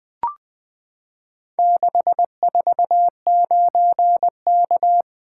Morse code, heard as a keyed tone, about 20 wpm, 700 Hz. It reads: 649K
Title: Four Letter-Number Combinations - 20wpm